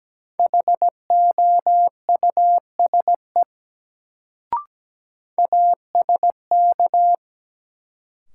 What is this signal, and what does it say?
Morse code, heard as a keyed tone, about 17 wpm, 700 Hz. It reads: HOUSE ASK
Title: Sentences from Top 200 Words - New Word Drill - 17wpm